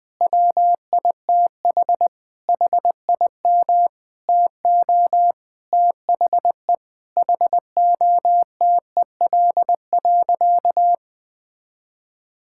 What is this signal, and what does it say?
Morse code, heard as a keyed tone, about 20 wpm, 700 Hz. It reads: WITH HIM TO THE HOTEL.
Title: Sentences from Top 1600 Words 20wpm